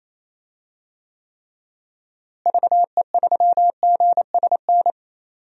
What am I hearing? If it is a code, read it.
Morse code, heard as a keyed tone, about 27 wpm, 700 Hz. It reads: VE3GSN
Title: DXCC-100 Call Signs - 6 Characters - Repeated 2x - 28wpm